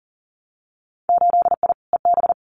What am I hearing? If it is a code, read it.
Morse code, heard as a keyed tone, about 42 wpm, 700 Hz. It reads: 8I EB
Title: Winter Field Day Exchanges Encore 40wpm